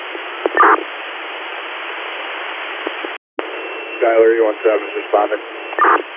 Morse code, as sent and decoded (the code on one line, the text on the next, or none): none